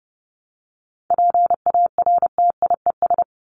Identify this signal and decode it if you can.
Morse code, heard as a keyed tone, about 28 wpm, 700 Hz. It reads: PARTIES
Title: ES:S Suffix - Top 100 Words with Base Words 30wpm